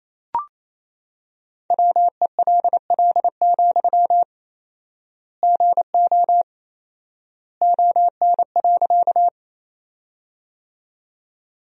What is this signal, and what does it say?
Morse code, heard as a keyed tone, about 28 wpm, 700 Hz. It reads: WELL, GO ON.
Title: 4x Word-Spacing - Sentences from Top 200 Words - 28wpm